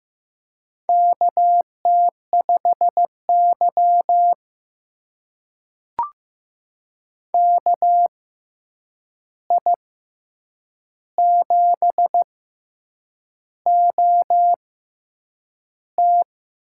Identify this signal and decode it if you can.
Morse code, heard as a keyed tone, about 15 wpm, spaced as for 7 wpm, 700 Hz. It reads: KT5Y KI7OT
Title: USA Call Signs - 6x Character Spacing - 15wpm